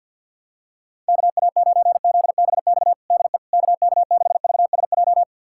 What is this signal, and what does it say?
Morse code, heard as a keyed tone, about 47 wpm, 700 Hz. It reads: KA9ZBX DE KK6VSJ
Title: USA Callsign DE Callsign - 6 Characters - 50wpm